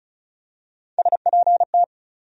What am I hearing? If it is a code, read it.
Morse code, heard as a keyed tone, about 36 wpm, 700 Hz. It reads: SPT
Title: USA Call Sign Postfixes - 3 Letters 35wpm